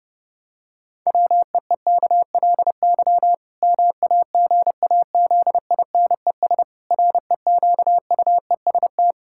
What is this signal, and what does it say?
Morse code, heard as a keyed tone, about 30 wpm, 700 Hz. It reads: WEEKLY MAGAZINES REQUEST
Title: Pangram Challenge v2 - 30wpm